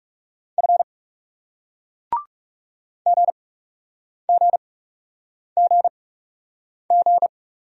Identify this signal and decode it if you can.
Morse code, heard as a keyed tone, about 41 wpm, spaced as for 8 wpm, 700 Hz. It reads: F GGGG